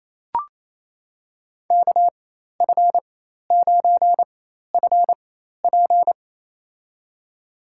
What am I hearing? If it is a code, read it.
Morse code, heard as a keyed tone, about 27 wpm, spaced as for 13 wpm, 700 Hz. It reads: KF9FP
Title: USA Call Signs - 5 Characters - 4x Character Spacing - 28wpm